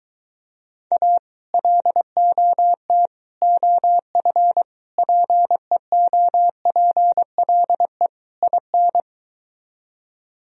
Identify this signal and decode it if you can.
Morse code, heard as a keyed tone, about 23 wpm, 700 Hz. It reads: A LOT OF PEOPLE IN